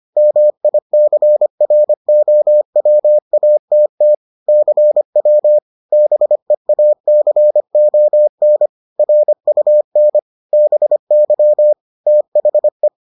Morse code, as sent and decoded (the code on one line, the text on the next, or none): -- .. -.-. .-. --- .-- .- - - / -.-. .-- / -... .- -.-. --- -. / .-. ..- -. / -... -.-- / - .... .
MICROWATT CW BEACON RUN BY THE